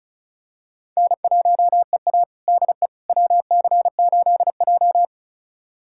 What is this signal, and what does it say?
Morse code, heard as a keyed tone, about 34 wpm, 700 Hz. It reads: N1EA DE WC8J
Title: USA Callsign DE Callsign - 4 Characters - 35wpm